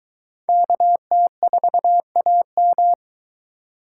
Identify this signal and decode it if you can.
Morse code, heard as a keyed tone, about 22 wpm, 700 Hz. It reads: KT4AM